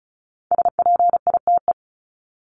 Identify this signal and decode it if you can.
Morse code, heard as a keyed tone, about 34 wpm, 700 Hz. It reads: SPITE
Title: Top 1751-2000 Words 35wpm